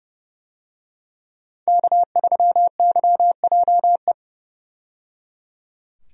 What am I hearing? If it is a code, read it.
Morse code, heard as a keyed tone, about 29 wpm, 700 Hz. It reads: K3YJE
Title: USA Call Signs - 5 Characters - 30wpm